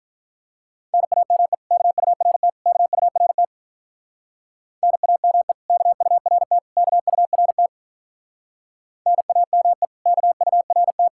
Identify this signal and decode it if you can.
Morse code, heard as a keyed tone, about 44 wpm, 700 Hz. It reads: NAME KURT KURT NAME KURT KURT NAME KURT